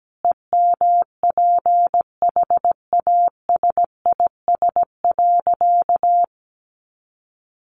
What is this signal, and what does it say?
Morse code, heard as a keyed tone, about 17 wpm, 700 Hz. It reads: EMPHASIS.
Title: Sentences from Top 1400 Words 17wpm